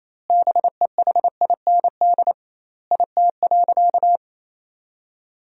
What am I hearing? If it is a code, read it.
Morse code, heard as a keyed tone, about 28 wpm, 700 Hz. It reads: BEHIND IT.
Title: 2x Word-Spacing - Sentences from Top 500 Words - 28wpm